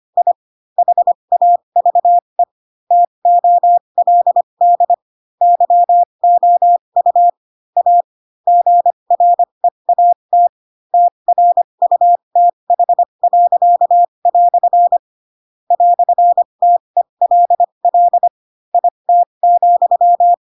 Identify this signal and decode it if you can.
Morse code, heard as a keyed tone, about 25 wpm, spaced as for 22 wpm, 700 Hz. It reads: I HAVE TOLD YOU A GREAT TRUTH." "TELL IT,
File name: Pinocchio / Pinocchio ch4 18 25 ik1hge